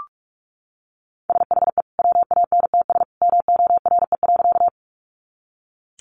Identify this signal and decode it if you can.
Morse code, heard as a keyed tone, about 44 wpm, 700 Hz. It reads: SHE WANTS MORE.
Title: Easy Phrases 45wpm